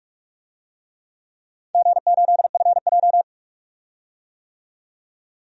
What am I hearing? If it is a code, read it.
Morse code, heard as a keyed tone, about 46 wpm, 700 Hz. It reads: G8FJ
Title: DXCC-100 Call Signs - 4 Characters - 45wpm